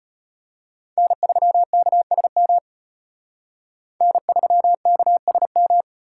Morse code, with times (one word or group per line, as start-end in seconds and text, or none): 1.0-2.6 N3KSM
4.0-5.8 N3KSM